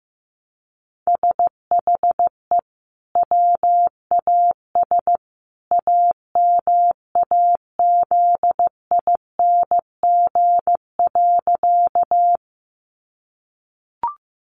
Morse code, heard as a keyed tone, about 15 wpm, 700 Hz. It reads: SHE WAS AMAZING.
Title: Sentences from Top 1500 Words 15wpm